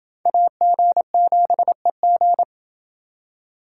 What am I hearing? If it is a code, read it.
Morse code, heard as a keyed tone, about 28 wpm, 700 Hz. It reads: AG7EG